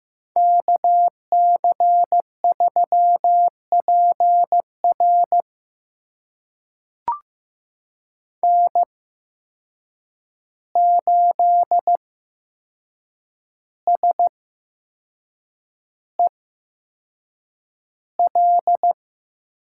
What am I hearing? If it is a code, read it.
Morse code, heard as a keyed tone, about 15 wpm, spaced as for 6 wpm, 700 Hz. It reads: KC3PR N8SEL
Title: USA Call Signs - 5 Characters - 8x Character Spacing - 15wpm